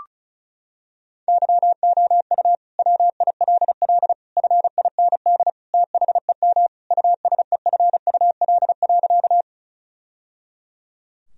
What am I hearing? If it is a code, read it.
Morse code, heard as a keyed tone, about 34 wpm, 700 Hz. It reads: YOU WILL FIND THEM USEFUL.
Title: Sentences from Top 1000 Words - Review Part 4 - 35wpm